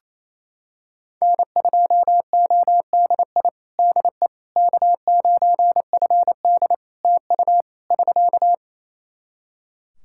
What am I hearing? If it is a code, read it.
Morse code, heard as a keyed tone, about 28 wpm, 700 Hz. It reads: N2ODI DE K9FD TU <SK>
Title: Practice QSO - Exchange 4 - Ending - 28wpm